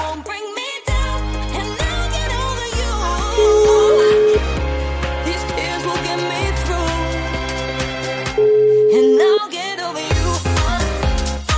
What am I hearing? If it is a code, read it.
Ringback tone (Europe, 425 Hz), with noise.